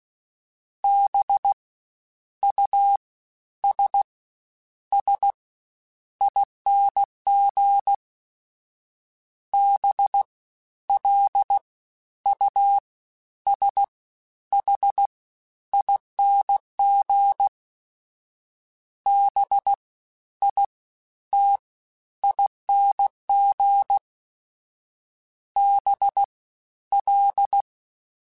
Morse code, heard as a keyed tone, about 16 wpm, 800 Hz. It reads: B U S S ING B L U S H ING B I T ING B L
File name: Special INGs-mp3s / Special INGs-mp3s / SlowToFaster ing @15